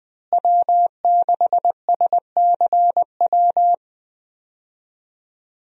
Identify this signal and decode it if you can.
Morse code, heard as a keyed tone, about 20 wpm, 700 Hz. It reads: W6SCW
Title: USA Call Signs - 5 Characters - Repeated 2x - 20wpm